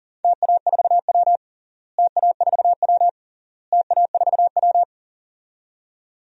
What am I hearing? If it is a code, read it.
Morse code, heard as a keyed tone, about 40 wpm, 700 Hz. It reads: TA4W TA4W TA4W
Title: DXCC-100 Call Signs - 4 Characters - Repeated 3x - 40wpm